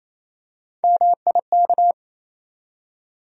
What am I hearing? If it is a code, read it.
Morse code, heard as a keyed tone, about 28 wpm, 700 Hz. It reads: MIK
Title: USA Call Sign Postfixes - 3 Letters 28wpm